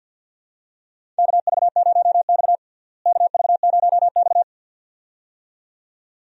Morse code, heard as a keyed tone, about 52 wpm, 700 Hz. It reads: KV0X KV0X
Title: USA Call Signs - 4 Characters - Repeated 2x - 50wpm